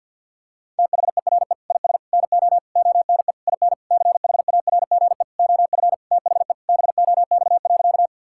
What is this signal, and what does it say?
Morse code, heard as a keyed tone, about 50 wpm, 700 Hz. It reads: THERE IS NO ONE IN CHARGE OF THE BOX.
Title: Sentences from Top 1000 Words - Review Part 6 - 50wpm